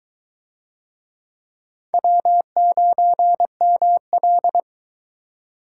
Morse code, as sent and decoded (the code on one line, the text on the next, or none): .-- ----. -- .-..
W9ML